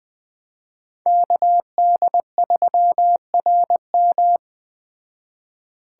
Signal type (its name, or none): Morse code